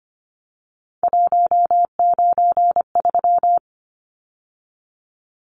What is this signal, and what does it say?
Morse code, heard as a keyed tone, about 25 wpm, 690 Hz. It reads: 193
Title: Numbers - 3-Digit 25wpm